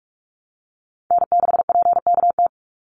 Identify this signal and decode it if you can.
Morse code, heard as a keyed tone, about 44 wpm, 700 Hz. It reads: N6PKT
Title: NA SOTA Chaser Callsigns 45wpm